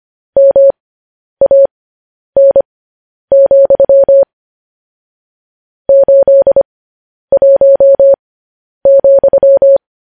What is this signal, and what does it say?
Morse code, heard as a keyed tone, about 25 wpm, spaced as for 10 wpm, 550 Hz. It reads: MAN, 81,
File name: podcast / News headlines-1x-25x10